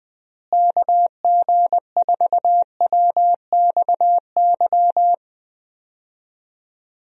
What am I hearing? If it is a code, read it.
Morse code, heard as a keyed tone, about 20 wpm, 700 Hz. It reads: KG4WXY
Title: USA Call Signs - 6 Characters - Repeated 3x - 20wpm